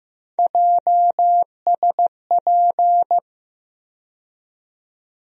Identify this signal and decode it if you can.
Morse code, heard as a keyed tone, about 15 wpm, 700 Hz. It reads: JSP
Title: USA Call Sign Postfixes - 3 Letters 15wpm